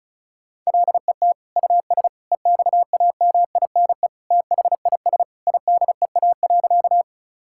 Morse code, heard as a keyed tone, about 34 wpm, 700 Hz. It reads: LET US EXAMINE THIS IDEA.
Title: Sentences from Top 1600 Words 35wpm